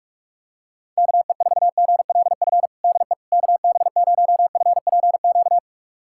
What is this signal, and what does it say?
Morse code, heard as a keyed tone, about 46 wpm, 700 Hz. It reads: KE4GLF DE KB0FPQ
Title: USA Callsign DE Callsign 45wpm